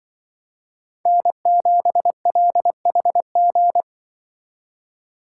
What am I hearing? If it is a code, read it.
Morse code, heard as a keyed tone, about 25 wpm, 700 Hz. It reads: N7LHG
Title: Speed Racing - USA 1x3 Call Signs 20wpm